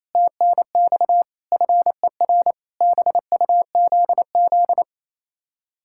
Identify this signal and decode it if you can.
Morse code, heard as a keyed tone, about 34 wpm, 700 Hz. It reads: TNX FER BUZZ
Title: Speed-Racing Common QSO Phrases - 28wpm